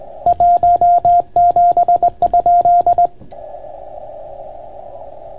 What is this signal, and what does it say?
Morse code, heard as a keyed tone, about 20 wpm, 700 Hz. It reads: 17?